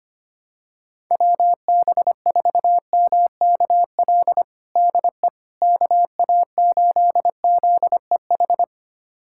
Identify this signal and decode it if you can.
Morse code, heard as a keyed tone, about 25 wpm, 700 Hz. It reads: WB4MKL DE KA8ZEH
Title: USA Callsign DE Callsign - 6 Characters - 25wpm